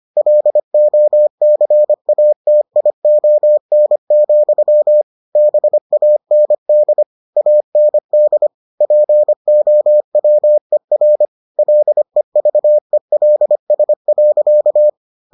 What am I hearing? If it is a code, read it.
Morse code, heard as a keyed tone, about 25 wpm, 600 Hz. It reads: LOCATION, BAND AND POWER LEVELS.